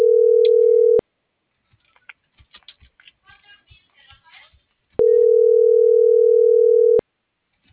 Ringback tone (North America), 440 + 480 Hz.